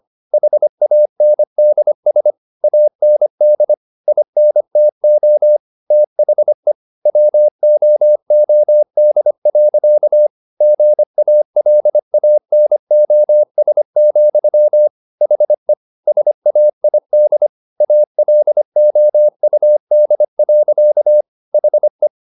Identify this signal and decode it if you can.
Morse code, heard as a keyed tone, about 25 wpm, 600 Hz. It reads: HANDS AND INTO THE WOOD. GALANOS, HE SAID ALOUD. HE